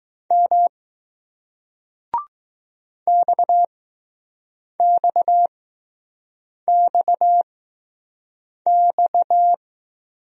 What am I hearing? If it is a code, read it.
Morse code, heard as a keyed tone, about 20 wpm, spaced as for 6 wpm, 700 Hz. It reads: M XXXX